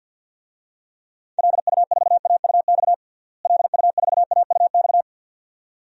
Morse code, heard as a keyed tone, about 49 wpm, 700 Hz. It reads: LU4AUX LU4AUX
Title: DXCC-100 Call Signs - 6 Characters - Repeated 2x - 50wpm